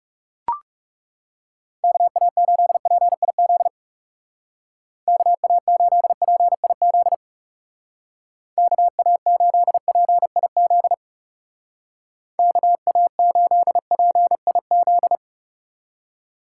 Morse code, heard as a keyed tone, about 36 wpm, 700 Hz. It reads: KA8PIZ KA8PIZ KA8PIZ KA8PIZ
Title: Speed Racing - USA 2x3 Call Signs 30wpm